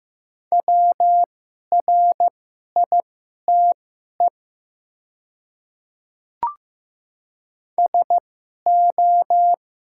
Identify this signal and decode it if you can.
Morse code, heard as a keyed tone, about 15 wpm, spaced as for 11 wpm, 700 Hz. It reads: WRITE SO
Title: 2x Character Spacing - Top 100 Words - 15wpm